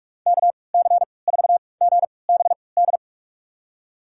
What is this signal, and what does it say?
Morse code, heard as a keyed tone, about 46 wpm, spaced as for 26 wpm, 700 Hz. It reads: KC4GBD
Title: USA Call Signs - 6 Characters - 3x Character Spacing - 45wpm